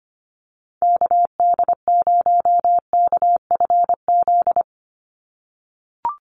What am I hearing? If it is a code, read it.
Morse code, heard as a keyed tone, about 25 wpm, 700 Hz. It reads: KD0KFZ
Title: USA Call Signs 25wpm